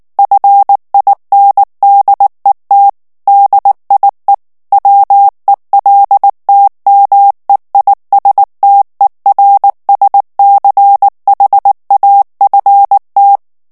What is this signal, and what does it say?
Morse code, heard as a keyed tone, about 19 wpm, 800 Hz. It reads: FINDET DIE WELTMEISTERSCHAFT
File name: morse mp3 / ALLEMAND texte clair 2 vitesse 18 mots minute